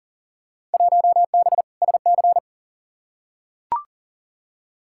Morse code, heard as a keyed tone, about 40 wpm, 700 Hz. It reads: 1B SC
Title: Field Day Exchanges Encore - 40wpm